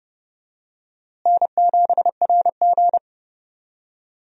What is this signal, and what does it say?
Morse code, heard as a keyed tone, about 32 wpm, 700 Hz. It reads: N7RG
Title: USA Call Signs - 4 Characters - Repeated 2x - 30wpm